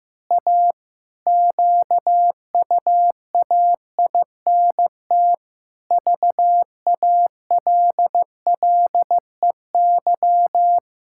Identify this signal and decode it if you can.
Morse code, heard as a keyed tone, about 15 wpm, 700 Hz. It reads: A QUAINT VALLEY
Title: Pangram Challenge v2 - 15wpm